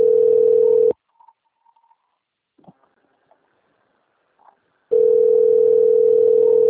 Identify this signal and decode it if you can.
Ringback tone (North America), 440 + 480 Hz.